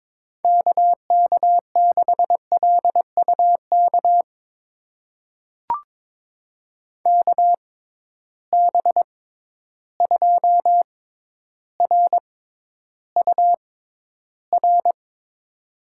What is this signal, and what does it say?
Morse code, heard as a keyed tone, about 22 wpm, spaced as for 11 wpm, 700 Hz. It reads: KK6LUK KB2RUR
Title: USA Call Signs - 6 Characters - 6x Character Spacing - 22wpm